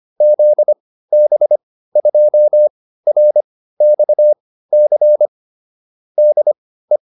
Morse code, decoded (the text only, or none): ZB2RXC DE